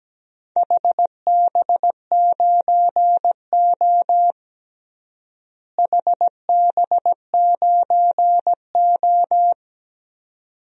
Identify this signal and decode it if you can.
Morse code, heard as a keyed tone, about 17 wpm, 700 Hz. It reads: HB9O HB9O
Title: DXCC-100 Call Signs - 4 Characters - Repeated 2x - 17wpm